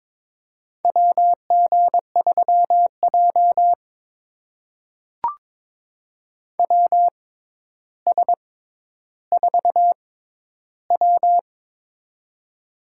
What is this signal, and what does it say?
Morse code, heard as a keyed tone, about 21 wpm, spaced as for 11 wpm, 700 Hz. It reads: WG3J WS4W
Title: USA Call Signs - 4 Characters - 6x Character Spacing - 22wpm